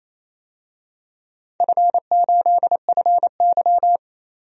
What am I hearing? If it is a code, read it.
Morse code, heard as a keyed tone, about 28 wpm, 700 Hz. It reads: F8FY
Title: DXCC-100 Call Signs - 4 Characters - 28wpm